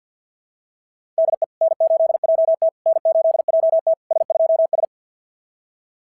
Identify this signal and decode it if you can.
Morse code, heard as a keyed tone, about 53 wpm, 690 Hz. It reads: DE N8JT N8JT S2S
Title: SOTA QSO Practice 50wpm